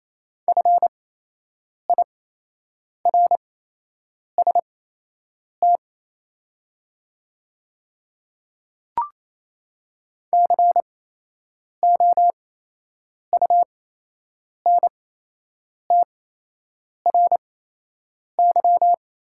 Morse code, heard as a keyed tone, about 28 wpm, spaced as for 8 wpm, 700 Hz. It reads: FIRST COUNTRY
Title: 8x Character Spacing - Top 500 Words - 28wpm